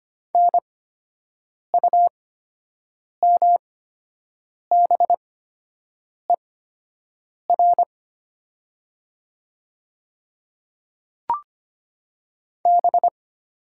Morse code, heard as a keyed tone, about 25 wpm, spaced as for 7 wpm, 700 Hz. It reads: NUMBER B
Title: 8x Character Spacing - Top 100 Words - 25wpm